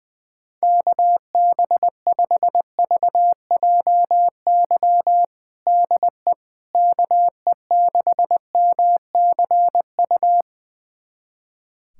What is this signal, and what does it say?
Morse code, heard as a keyed tone, about 20 wpm, 700 Hz. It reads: KB5VJY DE KE6MCU
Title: USA Callsign DE Callsign 20wpm